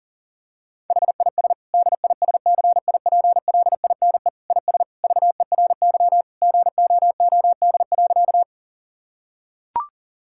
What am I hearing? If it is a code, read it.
Morse code, heard as a keyed tone, about 37 wpm, 700 Hz. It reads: HIS DISCIPLINE IS VERY GOOD.
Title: Sentences from Top 1000 Words - Review Part 6 - 40wpm